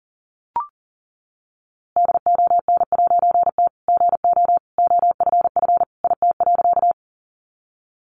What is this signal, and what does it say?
Morse code, heard as a keyed tone, about 40 wpm, 700 Hz. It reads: DON'T GO OFF IT.
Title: A Sound Of Thunder 40wpm